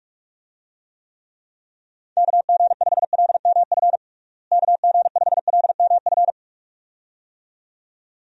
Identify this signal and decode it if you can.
Morse code, heard as a keyed tone, about 41 wpm, 700 Hz. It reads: KG5LMF KG5LMF
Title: USA Call Signs - 6 Characters - Repeated 2x - 45wpm